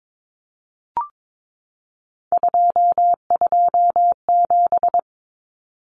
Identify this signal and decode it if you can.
Morse code, heard as a keyed tone, about 22 wpm, 700 Hz. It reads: 227
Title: Common Cut Numbers - 3-Digit 22wpm